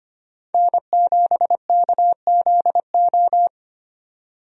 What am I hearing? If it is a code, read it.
Morse code, heard as a keyed tone, about 25 wpm, 700 Hz. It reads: N7KZO
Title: USA Call Signs - 5 Characters - Repeated 3x - 25wpm